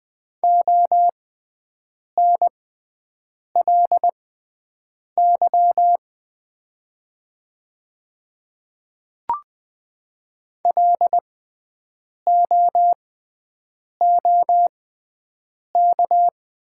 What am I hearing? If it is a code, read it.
Morse code, heard as a keyed tone, about 20 wpm, spaced as for 7 wpm, 700 Hz. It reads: ONLY LOOK
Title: 6x Character Spacing - Top 500 Words - 20wpm